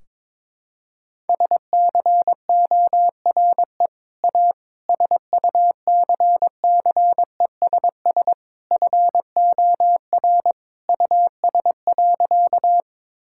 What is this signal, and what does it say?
Morse code, heard as a keyed tone, about 22 wpm, 700 Hz. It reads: SCORE A SUCCESS FOR US.
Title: Sentences from Top 1600 Words 22wpm